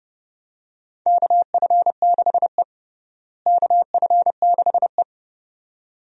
Morse code, heard as a keyed tone, about 30 wpm, 700 Hz. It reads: KF6E KF6E
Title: USA Call Signs - 4 Characters - Repeated 2x - 30wpm